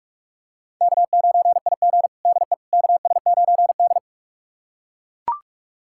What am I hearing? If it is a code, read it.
Morse code, heard as a keyed tone, about 41 wpm, 700 Hz. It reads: K9IG DE KS9D
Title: USA Callsign DE Callsign - 4 Characters - 45wpm